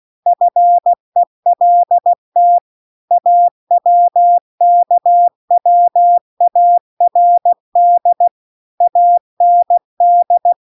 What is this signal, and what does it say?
Morse code, heard as a keyed tone, about 16 wpm, 700 Hz. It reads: FELT AWKWARD AND
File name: CW-The-Big-Twoheaded-River-2-16WPM